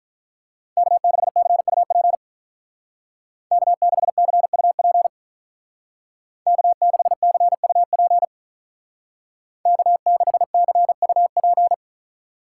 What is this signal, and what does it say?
Morse code, heard as a keyed tone, about 41 wpm, 700 Hz. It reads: K6CUP K6CUP K6CUP K6CUP
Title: Speed Racing - USA 1x3 Call Signs 35wpm